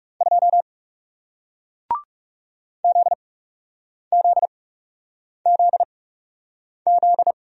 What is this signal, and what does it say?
Morse code, heard as a keyed tone, about 41 wpm, spaced as for 8 wpm, 700 Hz. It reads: 2 ZZZZ